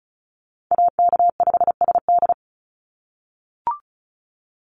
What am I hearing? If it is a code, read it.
Morse code, heard as a keyed tone, about 34 wpm, 700 Hz. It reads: AK5SD
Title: NA SOTA Chaser Callsigns 35wpm